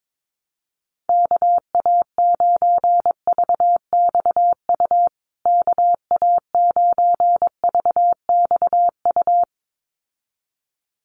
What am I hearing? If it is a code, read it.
Morse code, heard as a keyed tone, about 22 wpm, 700 Hz. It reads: KA9VXU KA9VXU
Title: USA Call Signs Repeated 2x 22wpm